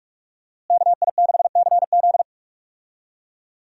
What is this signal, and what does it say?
Morse code, heard as a keyed tone, about 46 wpm, 700 Hz. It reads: KI6CZ
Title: USA Call Signs - 5 Characters - Repeated 2x - 45wpm